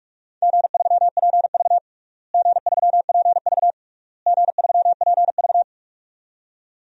Morse code, heard as a keyed tone, about 42 wpm, 700 Hz. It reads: G3PV G3PV G3PV
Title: DXCC-100 Call Signs - 4 Characters - Repeated 3x - 45wpm